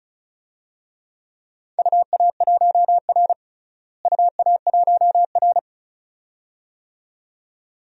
Morse code, heard as a keyed tone, about 36 wpm, 700 Hz. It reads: UA1R UA1R
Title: DXCC-100 Call Signs - 4 Characters - Repeated 2x - 35wpm